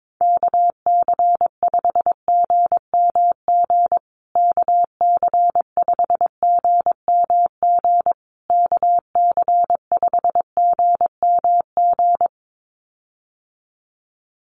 Morse code, heard as a keyed tone, about 22 wpm, 700 Hz. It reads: KC5GMG KC5GMG KC5GMG
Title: USA Call Signs Repeated 3x 22wpm